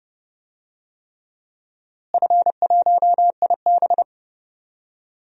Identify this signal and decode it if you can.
Morse code, heard as a keyed tone, about 31 wpm, 700 Hz. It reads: F1IB
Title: DXCC-100 Call Signs - 4 Characters - Repeated 3x - 30wpm